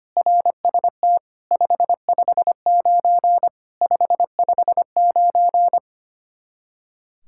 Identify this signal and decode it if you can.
Morse code, heard as a keyed tone, about 25 wpm, 700 Hz. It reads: RST 559 559
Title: Practice QSO - Exchange 1 - Beginning - 25wpm